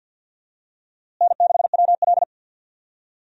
Morse code, heard as a keyed tone, about 50 wpm, 700 Hz. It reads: N6WL
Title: USA Call Signs - 4 Characters - Repeated 3x - 50wpm